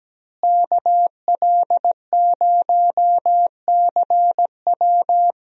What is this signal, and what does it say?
Morse code, heard as a keyed tone, about 17 wpm, 700 Hz. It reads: KL0CW
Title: USA Call Signs - 5 Characters - Repeated 3x - 17wpm